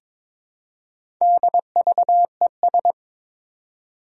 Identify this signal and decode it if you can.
Morse code, heard as a keyed tone, about 23 wpm, 700 Hz. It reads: DVES
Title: Four Challenging Letter-Number Combinations 22wpm